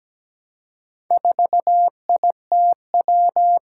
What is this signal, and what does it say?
Morse code, heard as a keyed tone, about 17 wpm, 700 Hz. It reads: 4ITW
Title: Four Challenging Letter-Number Combinations 17wpm